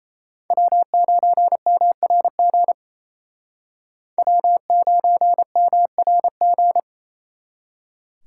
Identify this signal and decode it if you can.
Morse code, heard as a keyed tone, about 36 wpm, 700 Hz. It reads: W9MRG W9MRG